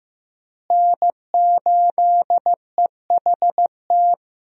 Morse code, heard as a keyed tone, about 16 wpm, 700 Hz. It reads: N8EHT
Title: USA Call Signs - 5 Characters - Repeated 3x - 15wpm